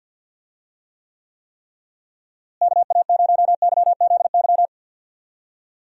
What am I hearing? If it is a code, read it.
Morse code, heard as a keyed tone, about 53 wpm, 700 Hz. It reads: KA0YZY
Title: USA Call Signs - 6 Characters - Repeated 3x - 50wpm